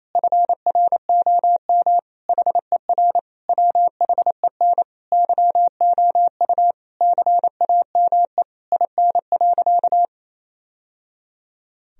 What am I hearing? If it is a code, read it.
Morse code, heard as a keyed tone, about 28 wpm, 700 Hz. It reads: FROM HER WHEN YOU CAME IN.
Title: Sentences from Top 1000 Words - Review Part 3 - 28wpm